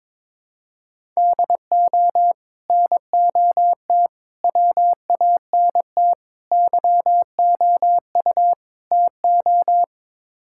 Morse code, heard as a keyed tone, about 22 wpm, 700 Hz. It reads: DO NOT WANT YOU TO
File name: practice / 22wpm / Build A Phrase - 5-Word Phrases - 22wpm